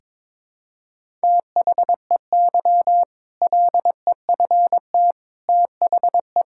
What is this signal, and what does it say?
Morse code, heard as a keyed tone, about 22 wpm, 700 Hz. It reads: THEY LEFT THE
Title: Sentences from Top 1000 Words - Review Part 5 - 22wpm